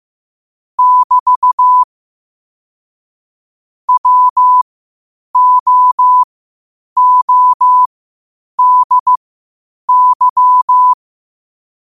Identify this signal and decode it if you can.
Morse code, heard as a keyed tone, about 15 wpm, spaced as for 8 wpm, 1000 Hz. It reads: <BT> WOODY
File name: qotdfiles / 8wpm